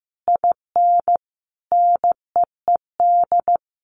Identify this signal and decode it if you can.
Morse code, heard as a keyed tone, about 15 wpm, 700 Hz. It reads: IN NEED
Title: Sentences from Top 1200 Words 15wpm